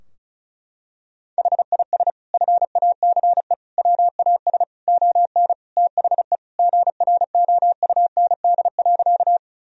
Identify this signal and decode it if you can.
Morse code, heard as a keyed tone, about 36 wpm, 700 Hz. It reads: HIS FACE WAS ON THE GROUND.
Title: Sentences from Top 1000 Words - Review Part 2 - 35wpm